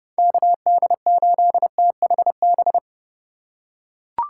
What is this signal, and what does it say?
Morse code, heard as a keyed tone, about 30 wpm, 700 Hz. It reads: KD8THB
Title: USA Call Signs - 6 Characters - 4x Character Spacing - 30wpm